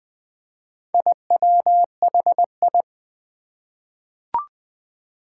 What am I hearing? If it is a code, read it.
Morse code, heard as a keyed tone, about 20 wpm, 700 Hz. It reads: IWHI
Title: Four Challenging Letter-Number Combinations 20wpm